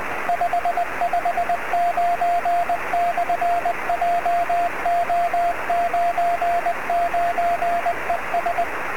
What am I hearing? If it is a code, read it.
Morse code, heard as a keyed tone, about 20 wpm, 700 Hz, with radio noise. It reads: 559/JO99ES
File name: sm7lcb / 02-sm0eui 23cm